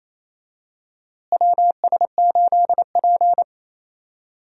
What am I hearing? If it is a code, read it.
Morse code, heard as a keyed tone, about 27 wpm, 700 Hz. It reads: WS8P
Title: USA Call Signs - 4 Characters - Repeated 2x - 28wpm